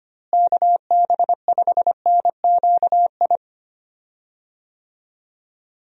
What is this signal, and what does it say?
Morse code, heard as a keyed tone, about 25 wpm, 700 Hz. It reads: KB5NQI
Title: USA Call Signs - 6 Characters - Repeated 2x - 25wpm